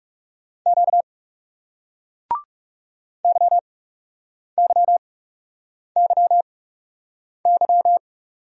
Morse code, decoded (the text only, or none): Q YYYY